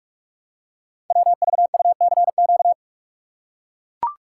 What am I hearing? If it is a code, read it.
Morse code, heard as a keyed tone, about 44 wpm, 700 Hz. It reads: WVUCQ
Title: Code Groups - 5 Letters - 45wpm